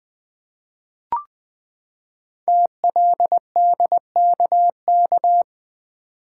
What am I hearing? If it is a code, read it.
Morse code, heard as a keyed tone, about 20 wpm, 700 Hz. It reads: TLDKK
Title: Code Groups - 5 Letters - 20wpm